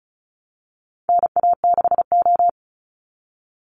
Morse code, heard as a keyed tone, about 34 wpm, 700 Hz. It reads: NA6O
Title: Call Signs - 35wpm